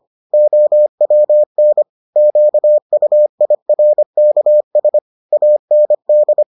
Morse code, heard as a keyed tone, about 25 wpm, 600 Hz. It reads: OWN QUIRKS AND